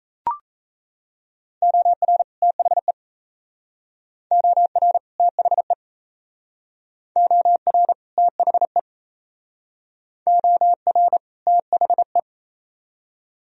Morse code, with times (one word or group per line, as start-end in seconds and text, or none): 1.6-2.2 OR
2.4-2.9 THE
4.3-5.0 OR
5.2-5.7 THE
7.2-7.9 OR
8.2-8.8 THE
10.3-11.2 OR
11.5-12.2 THE